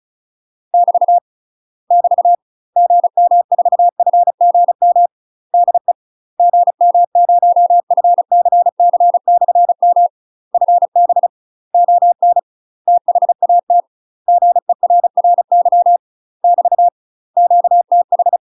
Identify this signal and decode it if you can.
Morse code, heard as a keyed tone, about 34 wpm, 700 Hz. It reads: <BT> <BT> GM4FGM DE GM0FCC/M FB ON THAT GERRY <BT> QTH
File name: UK MASTER QSOs TRAINING / 35WPM / UK MASTER QSOs 0030